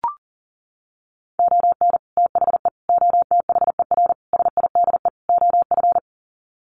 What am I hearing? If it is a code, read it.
Morse code, heard as a keyed tone, about 40 wpm, 700 Hz. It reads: ON THE OTHER SIDE OF
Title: Sets of 5 Words 40wpm